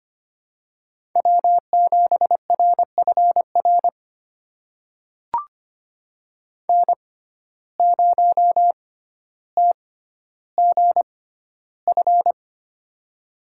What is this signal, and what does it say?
Morse code, heard as a keyed tone, about 25 wpm, spaced as for 13 wpm, 700 Hz. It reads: W7RFR N0TGF
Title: USA Call Signs - 5 Characters - 6x Character Spacing - 25wpm